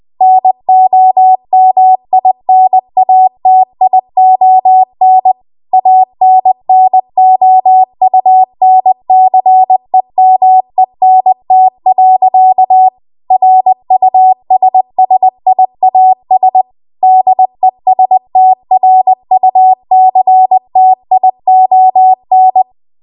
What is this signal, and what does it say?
Morse code, heard as a keyed tone, about 20 wpm, 750 Hz. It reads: NOMINATION ANNOUNCEMENT. RUSSIAS DESTRUCTION
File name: Archive / 20 WPM / 20WPM